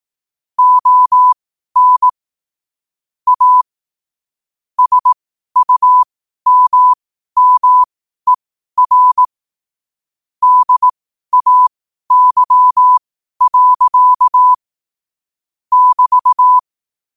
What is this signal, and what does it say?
Morse code, heard as a keyed tone, about 18 wpm, spaced as for 12 wpm, 1000 Hz. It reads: ON A SUMMER DAY. <BT>
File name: qotdfiles / 12wpm